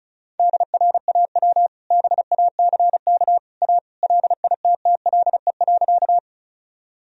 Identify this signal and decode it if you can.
Morse code, heard as a keyed tone, about 36 wpm, 700 Hz. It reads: DRAW BACK A LITTLE.
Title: Sentences from Top 1000 Words - Review Part 1 - 35wpm